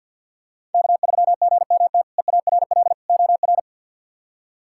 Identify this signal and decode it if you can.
Morse code, heard as a keyed tone, about 53 wpm, 700 Hz. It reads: K3GMT EARL OR
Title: K1USN SST QSO Practice - 50wpm